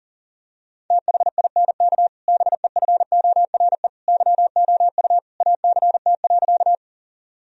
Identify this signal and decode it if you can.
Morse code, heard as a keyed tone, about 40 wpm, 700 Hz. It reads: THINK BEFORE YOU ACT.
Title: Sentences from Top 1000 Words - Review Part 1 - 40wpm